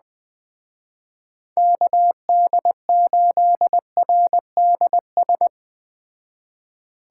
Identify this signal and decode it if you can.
Morse code, heard as a keyed tone, about 20 wpm, 700 Hz. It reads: KD8RDS
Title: Speed Racing - USA 2x3 Call Signs 17wpm